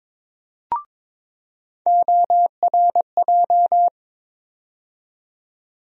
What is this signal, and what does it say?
Morse code, heard as a keyed tone, about 22 wpm, 700 Hz. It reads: ORJ
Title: USA Call Sign Postfixes - 3 Letters 22wpm